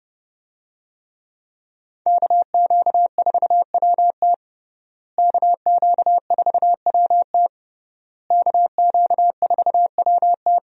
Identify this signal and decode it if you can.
Morse code, heard as a keyed tone, about 30 wpm, 700 Hz. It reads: KQ4WT KQ4WT KQ4WT
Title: USA Call Signs - 5 Characters - Repeated 3x - 30wpm